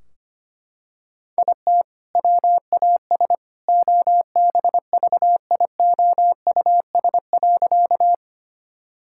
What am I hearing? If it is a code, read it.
Morse code, heard as a keyed tone, about 25 wpm, 700 Hz. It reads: IT WAS OBVIOUS.
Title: Sentences from Top 1600 Words 25wpm